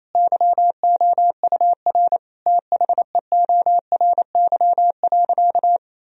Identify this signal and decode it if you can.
Morse code, heard as a keyed tone, about 28 wpm, 700 Hz. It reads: YOUR THEORY.
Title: Sentences from Top 1000 Words - Review Part 3 - 28wpm